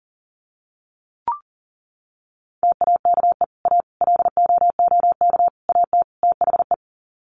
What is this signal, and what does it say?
Morse code, heard as a keyed tone, about 40 wpm, 700 Hz. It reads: TAKE A LOOK AT THE
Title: Sets of 5 Words 40wpm